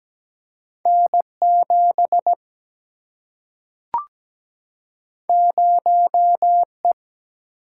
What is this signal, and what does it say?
Morse code, heard as a keyed tone, about 17 wpm, 700 Hz. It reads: N7 0E
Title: Single-Letter Single-Number 17wpm